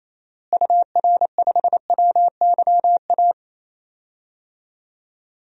Morse code, heard as a keyed tone, about 28 wpm, 700 Hz. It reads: UR5WYA
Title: DXCC-100 Call Signs - 6 Characters - Repeated 2x - 28wpm